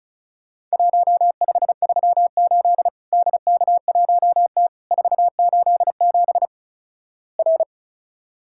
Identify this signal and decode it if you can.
Morse code, heard as a keyed tone, about 36 wpm, 700 Hz. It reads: 1538 DK1T 487 R
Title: Worked All Europe DX Contest - QTC Exchange - 35wpm